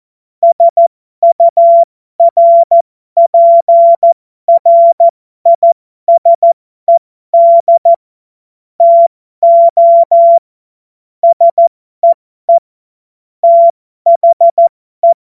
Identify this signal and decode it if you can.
Morse code, heard as a keyed tone, about 14 wpm, spaced as for 12 wpm, 680 Hz. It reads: SURPRISED TO SEE THE
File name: Morsekurs / Witze